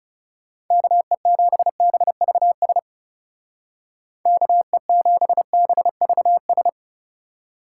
Morse code, heard as a keyed tone, about 35 wpm, 700 Hz. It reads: KE7BVS KE7BVS
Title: Speed Racing - USA Call Signs 30wpm